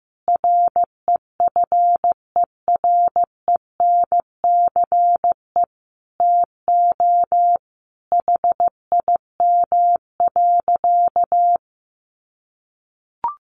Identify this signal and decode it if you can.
Morse code, heard as a keyed tone, about 15 wpm, 700 Hz. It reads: REFERENCE TO HIM.
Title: Sentences from Top 1200 Words 15wpm